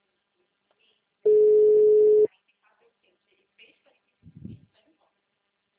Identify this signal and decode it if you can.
Ringback tone (Europe, 425 Hz), with speech.